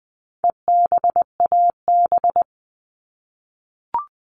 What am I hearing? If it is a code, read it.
Morse code, heard as a keyed tone, about 20 wpm, 700 Hz. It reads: EBAB
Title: All Cut Numbers - 4-Digit 20wpm